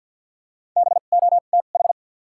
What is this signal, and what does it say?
Morse code, heard as a keyed tone, about 50 wpm, spaced as for 36 wpm, 700 Hz. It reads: BOTH
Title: 2x Character Spacing - Top 500 Words - 50wpm